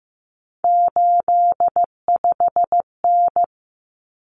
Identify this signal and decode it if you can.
Morse code, heard as a keyed tone, about 15 wpm, 700 Hz. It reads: 85N
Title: Common Cut Numbers - 3-Digit 15wpm